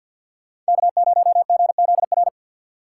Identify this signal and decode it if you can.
Morse code, heard as a keyed tone, about 50 wpm, 700 Hz. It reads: K0GZR
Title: USA Call Signs - 4x Character Spacing - 50wpm